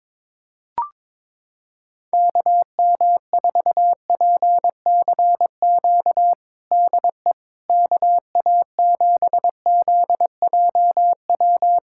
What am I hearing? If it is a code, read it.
Morse code, heard as a keyed tone, about 22 wpm, 700 Hz. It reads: KM4PCQ DE KA7ZJW
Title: USA Callsign DE Callsign - 6 Characters - 22wpm